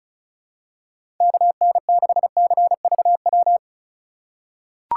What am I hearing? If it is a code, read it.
Morse code, heard as a keyed tone, about 36 wpm, 700 Hz. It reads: KN6CVW
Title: USA Call Signs - 3x Character Spacing - 35wpm